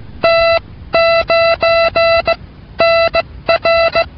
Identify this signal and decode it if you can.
Morse code, heard as a keyed tone, about 16 wpm, 670 Hz. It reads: T 9 N R